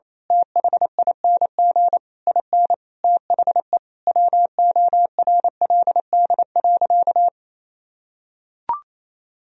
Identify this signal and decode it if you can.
Morse code, heard as a keyed tone, about 28 wpm, 700 Hz. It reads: THING IN THE WORLD.
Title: Sentences from Top 1000 Words - Review Part 3 - 28wpm